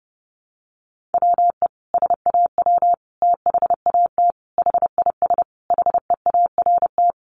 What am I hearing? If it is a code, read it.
Morse code, heard as a keyed tone, about 30 wpm, 700 Hz. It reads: WE SAW THAT HIS HEART
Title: Sentences from Top 1200 Words 30wpm